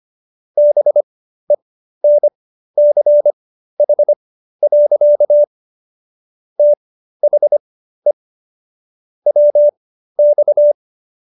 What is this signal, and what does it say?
Morse code, heard as a keyed tone, about 25 wpm, spaced as for 13 wpm, 590 Hz. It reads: BENCH. THE WX